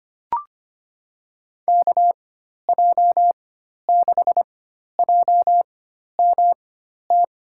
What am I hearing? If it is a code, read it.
Morse code, heard as a keyed tone, about 25 wpm, spaced as for 12 wpm, 700 Hz. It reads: KJ6JMT
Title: USA Call Signs - 6 Characters - 4x Character Spacing - 25wpm